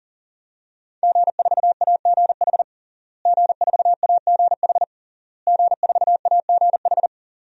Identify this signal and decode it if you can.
Morse code, heard as a keyed tone, about 40 wpm, 700 Hz. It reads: G4AGH G4AGH G4AGH
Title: DXCC-100 Call Signs - 5 Characters - Repeated 3x - 40wpm